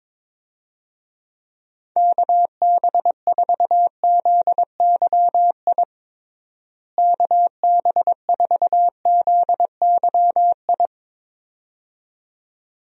Morse code, heard as a keyed tone, about 23 wpm, 700 Hz. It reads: KB4ZYI KB4ZYI
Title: USA Call Signs - 6 Characters - Repeated 2x - 22wpm